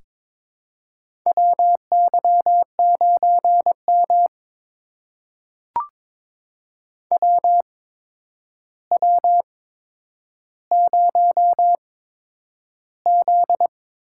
Morse code, heard as a keyed tone, about 23 wpm, spaced as for 10 wpm, 700 Hz. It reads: WY9M WW0Z
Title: USA Call Signs - 4 Characters - 8x Character Spacing - 22wpm